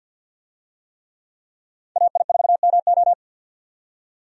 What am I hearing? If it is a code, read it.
Morse code, heard as a keyed tone, about 51 wpm, 700 Hz. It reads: AI4MO
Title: USA Call Signs - 5 Characters - 50wpm